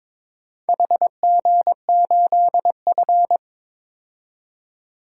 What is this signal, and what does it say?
Morse code, heard as a keyed tone, about 22 wpm, 700 Hz. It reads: HG8F
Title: DXCC-100 Call Signs - 4 Characters - 22wpm